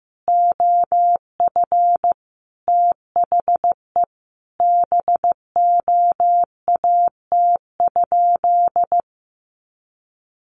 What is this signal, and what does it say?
Morse code, heard as a keyed tone, about 15 wpm, 700 Hz. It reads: OF THE BOAT?
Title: Sentences from Top 1300 Words 15wpm